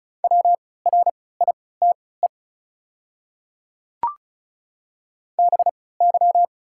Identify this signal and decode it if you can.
Morse code, heard as a keyed tone, about 36 wpm, spaced as for 20 wpm, 700 Hz. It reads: WRITE BY
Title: 3x Character Spacing - Top 500 Words - 35wpm